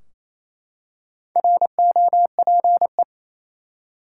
Morse code, heard as a keyed tone, about 28 wpm, 700 Hz. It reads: ROPE
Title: Four Letter Words - 28wpm